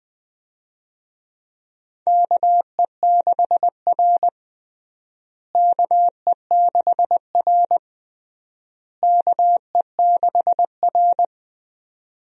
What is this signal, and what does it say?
Morse code, heard as a keyed tone, about 20 wpm, 700 Hz. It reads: KE6R KE6R KE6R
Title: USA Call Signs - 4 Characters - Repeated 3x - 20wpm